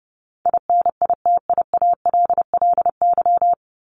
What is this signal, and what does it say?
Morse code, heard as a keyed tone, about 30 wpm, 700 Hz. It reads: INITIALLY
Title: Top 1251-1500 Words 30wpm